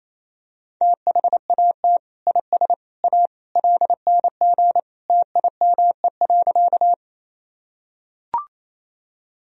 Morse code, heard as a keyed tone, about 28 wpm, 700 Hz. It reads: THAT IS A LNG TIME.
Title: Sentences from Top 100 Words and Abbreviations 28wpm